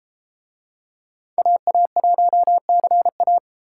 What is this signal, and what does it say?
Morse code, heard as a keyed tone, about 32 wpm, 700 Hz. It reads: AA1CA